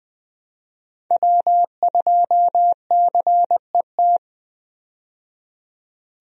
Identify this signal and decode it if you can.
Morse code, heard as a keyed tone, about 20 wpm, 700 Hz. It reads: W2CET
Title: USA Call Signs - 5 Characters - 20wpm